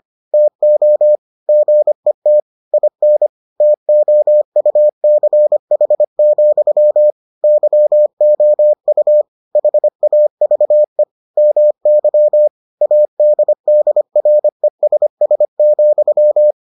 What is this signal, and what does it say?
Morse code, heard as a keyed tone, about 25 wpm, 600 Hz. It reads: TO GET IN TOUCH, YOU HAVE MY ADDRESS,